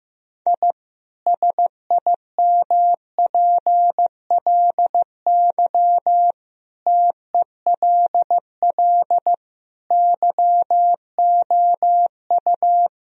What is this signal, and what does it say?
Morse code, heard as a keyed tone, about 14 wpm, 700 Hz. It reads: I SIMPLY TELL YOU
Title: Sentences from Top 1000 Words - Review Part 4 - 15wpm